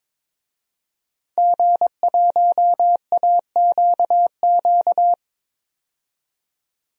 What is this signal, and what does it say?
Morse code, heard as a keyed tone, about 21 wpm, 700 Hz. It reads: G1AQQ
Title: DXCC-100 Call Signs - 5 Characters - 22wpm